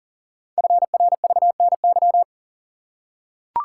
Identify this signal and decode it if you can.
Morse code, heard as a keyed tone, about 42 wpm, 690 Hz. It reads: FRVNY
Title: Code Groups - 5 Letters - 40wpm